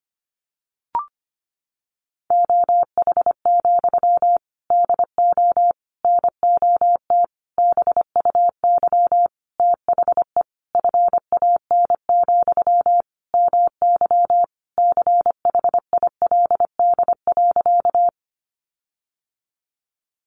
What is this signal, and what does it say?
Morse code, heard as a keyed tone, about 25 wpm, 700 Hz. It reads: OH, DO NOT BUY THE FAN, MY CHILD.
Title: Sentences from Top 1300 Words 25wpm